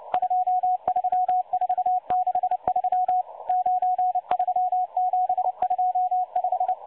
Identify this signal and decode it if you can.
Morse code, heard as a keyed tone, about 29 wpm, 730 Hz. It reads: 23463 93725
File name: M12-Morse-Numbers-Station-8116-kHz-CW